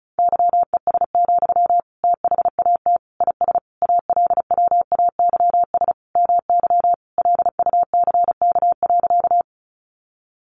Morse code, heard as a keyed tone, about 34 wpm, 700 Hz. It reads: YES, THAT IS ALWAYS MY LUCK.
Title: Sentences from Top 1500 Words 35wpm